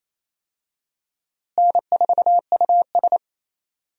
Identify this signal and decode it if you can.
Morse code, heard as a keyed tone, about 28 wpm, 700 Hz. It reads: N4US
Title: USA Call Signs - 4 Characters - Repeated 3x - 28wpm